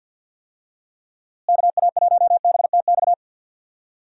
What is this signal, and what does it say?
Morse code, heard as a keyed tone, about 51 wpm, 700 Hz. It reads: KA1BTX
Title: USA Call Signs - 6 Characters - Repeated 2x - 50wpm